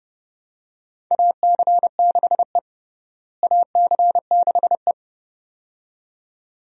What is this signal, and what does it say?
Morse code, heard as a keyed tone, about 31 wpm, 700 Hz. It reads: AC6E AC6E
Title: USA Call Signs - 4 Characters - Repeated 2x - 30wpm